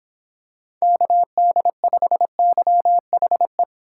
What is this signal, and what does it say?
Morse code, heard as a keyed tone, about 26 wpm, 700 Hz. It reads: KD5YHE